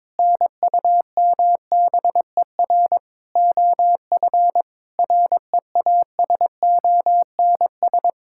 Morse code, heard as a keyed tone, about 22 wpm, 700 Hz. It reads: NUMBER OF REASONS
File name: practice / 22wpm / Sets of 8 Words - 22wpm